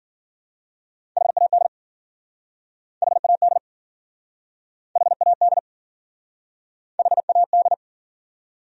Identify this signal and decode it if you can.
Morse code, heard as a keyed tone, about 52 wpm, 700 Hz. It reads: HAD HAD HAD HAD